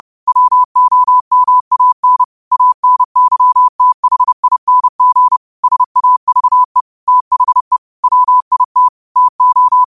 Morse code, heard as a keyed tone, about 30 wpm, 1000 Hz. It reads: WOMAN ANYTHING SAVE THE WIT TO
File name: qotdfiles / 30wpm